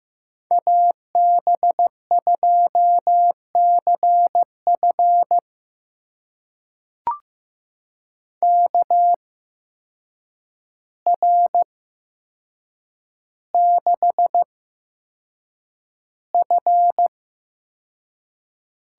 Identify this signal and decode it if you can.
Morse code, heard as a keyed tone, about 15 wpm, spaced as for 7 wpm, 700 Hz. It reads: AB2CF KR6F
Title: USA Call Signs - 5 Characters - 8x Character Spacing - 15wpm